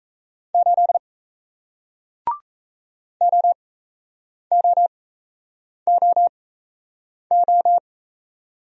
Morse code, heard as a keyed tone, about 37 wpm, spaced as for 8 wpm, 700 Hz. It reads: 8 OOOO